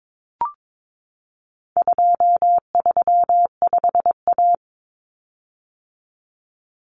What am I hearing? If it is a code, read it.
Morse code, heard as a keyed tone, about 22 wpm, 700 Hz. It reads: 235A
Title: Common Cut Numbers - 4-Digit 22wpm